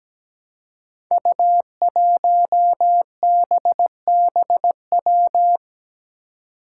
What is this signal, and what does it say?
Morse code, heard as a keyed tone, about 17 wpm, 700 Hz. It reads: U1BBW
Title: Five Challenging Letter-Number Combinations 17wpm